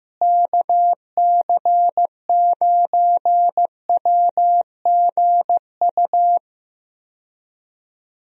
Morse code, heard as a keyed tone, about 15 wpm, 700 Hz. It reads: KC9WGU
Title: USA Call Signs - 6 Characters - Repeated 3x - 15wpm